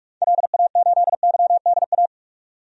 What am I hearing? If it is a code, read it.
Morse code, heard as a keyed tone, about 46 wpm, 700 Hz. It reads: LA8YDA
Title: DXCC-100 Call Signs - 6 Characters - 45wpm